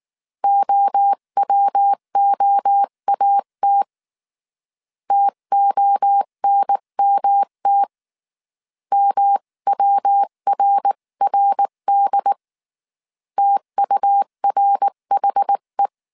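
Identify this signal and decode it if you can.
Morse code, heard as a keyed tone, about 19 wpm, 800 Hz. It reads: OWOAT TONMT MWRRD TURHE
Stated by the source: Morse code dictation at 18 wpm from the IW3SGT QRS Morse code training course, five-character groups of 300 characters.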